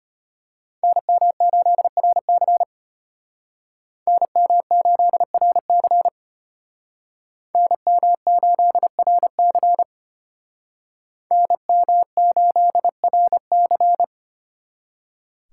Morse code, heard as a keyed tone, about 33 wpm, 700 Hz. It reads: NM8RC NM8RC NM8RC NM8RC